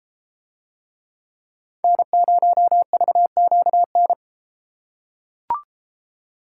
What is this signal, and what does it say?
Morse code, heard as a keyed tone, about 30 wpm, 700 Hz. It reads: N0VQN